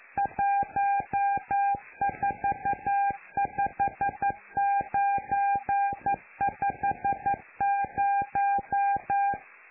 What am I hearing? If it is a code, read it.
Morse code, heard as a keyed tone, about 11 wpm, 800 Hz. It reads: 145950
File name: lu4aao / Globo Amsat